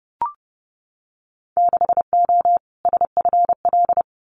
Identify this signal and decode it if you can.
Morse code, heard as a keyed tone, about 30 wpm, 700 Hz. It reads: 6O SFL
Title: Winter Field Day Exchanges Encore 30wpm